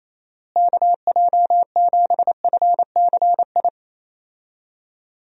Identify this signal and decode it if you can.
Morse code, heard as a keyed tone, about 28 wpm, 700 Hz. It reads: KJ7FCI
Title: USA Call Signs - 6 Characters - Repeated 3x - 28wpm